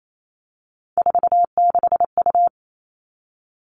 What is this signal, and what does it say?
Morse code, heard as a keyed tone, about 28 wpm, 700 Hz. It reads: 46U
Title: All Cut Numbers - 3-Digit 28wpm